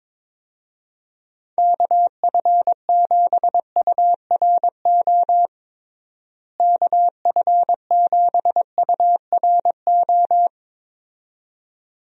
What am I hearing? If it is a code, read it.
Morse code, heard as a keyed tone, about 22 wpm, 700 Hz. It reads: KF7URO KF7URO
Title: USA Call Signs - 6 Characters - Repeated 2x - 22wpm